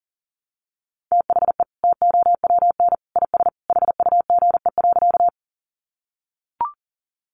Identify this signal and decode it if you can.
Morse code, heard as a keyed tone, about 40 wpm, 700 Hz. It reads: THE TOWN IS HUGE.
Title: Sentences from Top 1100 Words 40wpm